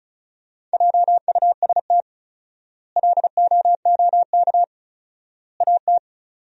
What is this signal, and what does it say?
Morse code, heard as a keyed tone, about 36 wpm, 700 Hz. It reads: JUST LOOK AT
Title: 4x Word-Spacing - Sentences from Top 200 Words - 35wpm